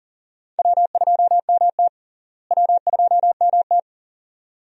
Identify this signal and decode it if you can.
Morse code, heard as a keyed tone, about 38 wpm, 700 Hz. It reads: W2MT W2MT
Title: USA Call Signs - 4 Characters - Repeated 2x - 40wpm